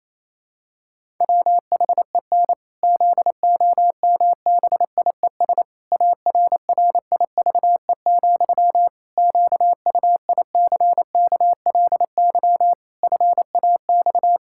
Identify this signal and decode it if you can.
Morse code, heard as a keyed tone, about 27 wpm, 700 Hz. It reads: WHEN ZOMBIES ARRIVE, QUICKLY FAX
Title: Pangram Challenge v2 - 28wpm